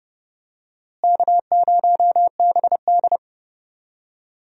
Morse code, heard as a keyed tone, about 32 wpm, 700 Hz. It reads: K0BD
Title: USA Call Signs - 4 Characters - Repeated 2x - 30wpm